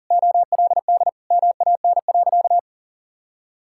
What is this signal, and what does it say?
Morse code, heard as a keyed tone, about 37 wpm, 700 Hz. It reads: OLD MAN.
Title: Sentences from Top 1600 Words 40wpm